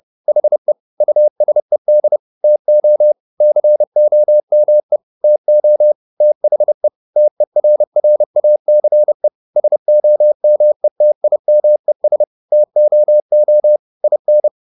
Morse code, heard as a keyed tone, about 30 wpm, 600 Hz. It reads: HE USED TO COME TO THE TERRACE SOMETIMES TOO IN
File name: the old man and the sea / the old man and the sea 1 30wpm